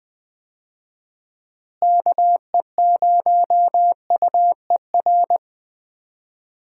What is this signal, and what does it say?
Morse code, heard as a keyed tone, about 20 wpm, 700 Hz. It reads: KE0UER
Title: USA Call Signs - 6 Characters - Repeated 3x - 20wpm